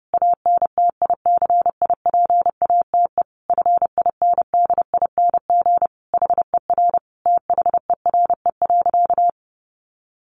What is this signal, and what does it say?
Morse code, heard as a keyed tone, about 30 wpm, 700 Hz. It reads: ANTICIPATE FINDING HER THERE.
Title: Sentences from Top 1500 Words 30wpm